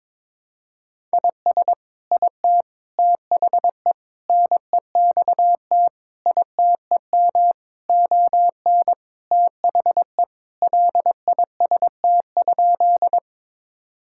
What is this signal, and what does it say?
Morse code, heard as a keyed tone, about 21 wpm, 700 Hz. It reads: IS IT THE NEXT ITEM ON THE LIST?
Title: Sentences from Top 1000 Words - Review Part 6 - 22wpm